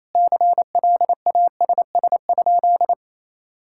Morse code, heard as a keyed tone, about 28 wpm, 700 Hz. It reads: CLASS?
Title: Sentences from Top 1000 Words - Review Part 5 - 28wpm